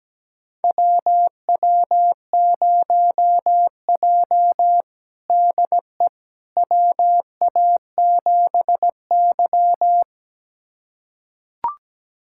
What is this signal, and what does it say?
Morse code, heard as a keyed tone, about 17 wpm, 700 Hz. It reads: WW0J DE WA7Y
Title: USA Callsign DE Callsign - 4 Characters - 17wpm